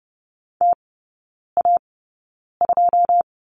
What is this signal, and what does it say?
Morse code, heard as a keyed tone, about 30 wpm, spaced as for 9 wpm, 700 Hz. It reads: TA2
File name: 081 Lesson 14e - Number 2 - Three Characters 30wpm